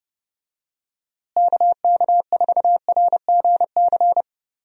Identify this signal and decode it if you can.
Morse code, heard as a keyed tone, about 31 wpm, 700 Hz. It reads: KK4RGC
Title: USA Call Signs - 6 Characters - Repeated 3x - 30wpm